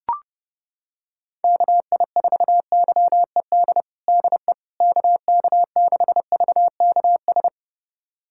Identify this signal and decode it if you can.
Morse code, heard as a keyed tone, about 30 wpm, 700 Hz. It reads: KI4YED DE KK6VKS
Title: USA Callsign DE Callsign - 6 Characters - 30wpm